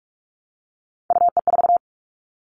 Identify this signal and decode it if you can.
Morse code, heard as a keyed tone, about 46 wpm, 700 Hz. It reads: UE4
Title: Three Challenging Letter-Number Combinations 45wpm